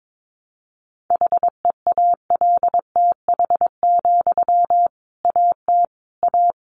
Morse code, heard as a keyed tone, about 22 wpm, 700 Hz. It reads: HEALTH, AT A
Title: Sentences from Top 600 Words 22wpm